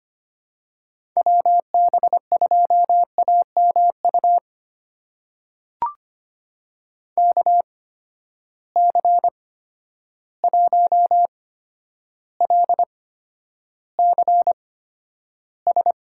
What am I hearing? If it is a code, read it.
Morse code, heard as a keyed tone, about 25 wpm, spaced as for 11 wpm, 700 Hz. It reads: WB2AMU KC1LCS
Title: USA Call Signs - 6 Characters - 8x Character Spacing - 25wpm